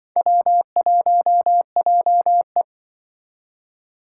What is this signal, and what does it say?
Morse code, heard as a keyed tone, about 24 wpm, 700 Hz. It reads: W1JE
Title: Speed Racing - USA 1x2 Call Signs 20wpm